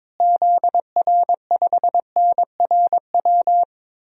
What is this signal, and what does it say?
Morse code, heard as a keyed tone, about 22 wpm, 700 Hz. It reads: ZR5NRW
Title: DXCC-100 Call Signs - 6 Characters - Repeated 3x - 22wpm